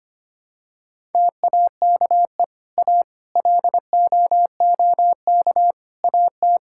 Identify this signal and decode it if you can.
Morse code, heard as a keyed tone, about 25 wpm, 700 Hz. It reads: TAKE A LOOK AT
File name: practice / 25wpm / Sentences with 4-Word Phrases - 25wpm